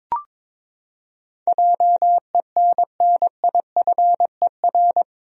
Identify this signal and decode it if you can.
Morse code, heard as a keyed tone, about 22 wpm, 700 Hz. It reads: JENNIFER
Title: USA Common Names - 22wpm